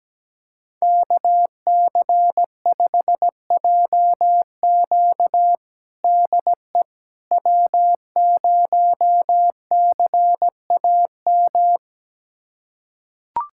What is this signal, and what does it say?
Morse code, heard as a keyed tone, about 17 wpm, 700 Hz. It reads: KC5JQ DE W0CAM
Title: USA Callsign DE Callsign - 5 Characters - 17wpm